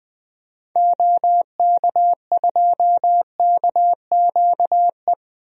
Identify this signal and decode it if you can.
Morse code, heard as a keyed tone, about 21 wpm, 700 Hz. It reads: OK2KQE
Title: DXCC-100 Call Signs - 6 Characters - Repeated 2x - 20wpm